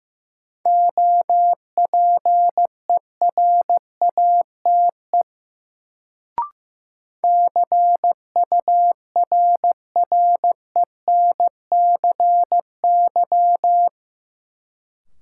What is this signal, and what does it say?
Morse code, heard as a keyed tone, about 16 wpm, 700 Hz. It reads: OPERATE CURRENCY
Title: Sentences from Top 1500 Words - New Word Drill Alt - 15wpm